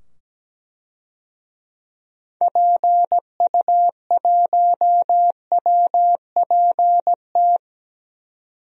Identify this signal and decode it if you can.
Morse code, heard as a keyed tone, about 17 wpm, 700 Hz. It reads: PU1WPT
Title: DXCC-100 Call Signs - 6 Characters - 17wpm